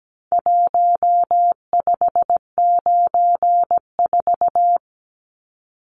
Morse code, heard as a keyed tone, about 17 wpm, 700 Hz. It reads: 1594
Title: Numbers - Contest Style - 4-Digit 17wpm